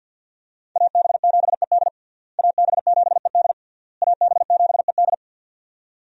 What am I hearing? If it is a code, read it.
Morse code, heard as a keyed tone, about 51 wpm, 700 Hz. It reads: AB7ED AB7ED AB7ED
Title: USA Call Signs - 5 Characters - Repeated 3x - 50wpm